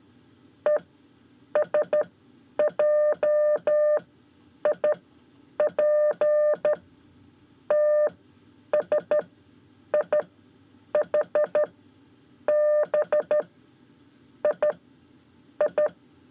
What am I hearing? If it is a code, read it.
Morse code, heard as a keyed tone, about 13 wpm, 600 Hz. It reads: E S J I P T S I H B I I
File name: 5 Lezione CW 1